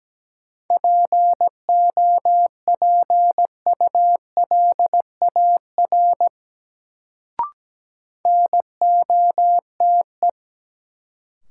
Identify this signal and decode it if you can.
Morse code, heard as a keyed tone, about 17 wpm, 700 Hz. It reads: POPULAR NOTE
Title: Sets of 5 Words - New Word Drill - 17wpm